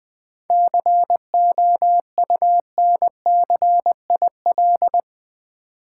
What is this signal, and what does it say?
Morse code, heard as a keyed tone, about 20 wpm, 700 Hz. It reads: COUNCIL
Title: 7 Letter Words - First 3 Letters Given - 20wpm